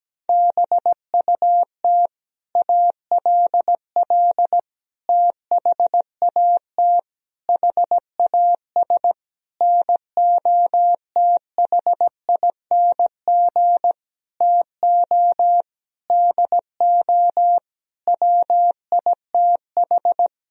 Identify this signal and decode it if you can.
Morse code, heard as a keyed tone, about 17 wpm, 700 Hz. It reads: BUT ALL THAT HAS NOTHING TO DO WITH
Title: Sentences from Top 1000 Words - Review Part 3 - 17wpm